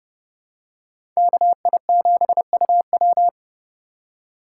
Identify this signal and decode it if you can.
Morse code, heard as a keyed tone, about 32 wpm, 700 Hz. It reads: KI7UW
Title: USA Call Signs - 5 Characters - 4x Character Spacing - 30wpm